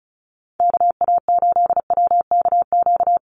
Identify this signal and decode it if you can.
Morse code, heard as a keyed tone, about 34 wpm, 700 Hz. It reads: KA8WKQ
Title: USA Call Signs Repeated 2x 35wpm